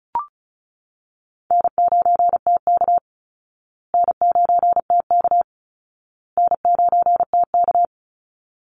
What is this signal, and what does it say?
Morse code, heard as a keyed tone, about 34 wpm, 700 Hz. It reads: N9TK N9TK N9TK
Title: Easy Call Signs Repeated 3x - 35wpm